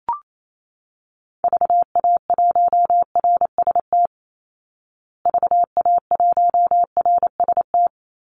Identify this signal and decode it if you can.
Morse code, heard as a keyed tone, about 28 wpm, 700 Hz. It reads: VA1RST VA1RST
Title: Call Signs Repeated 2x - 28wpm